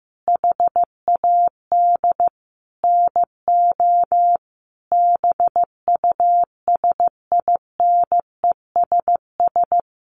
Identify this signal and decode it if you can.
Morse code, heard as a keyed tone, about 15 wpm, 700 Hz. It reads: HAD NO BUSINESS
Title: Sentences from Top 1300 Words 15wpm